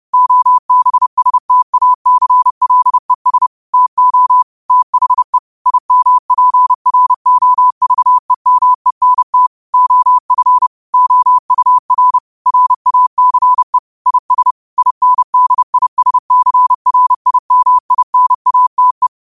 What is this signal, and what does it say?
Morse code, heard as a keyed tone, about 30 wpm, 1000 Hz. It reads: OBSTACLES TO THE IMPROVEMENT OF OUR RACE IS INDISCRIMINATE